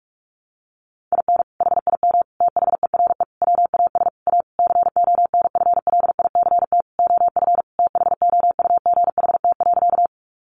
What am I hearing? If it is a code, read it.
Morse code, heard as a keyed tone, about 44 wpm, 700 Hz. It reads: IN HIM THERE WAS A CONFLICT OF THOUGHT.
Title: Sentences from Top 1300 Words 45wpm